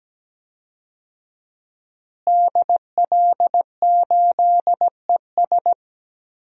Morse code, heard as a keyed tone, about 17 wpm, 700 Hz. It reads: DL8ES
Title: DXCC-100 Call Signs - 5 Characters - Repeated 3x - 17wpm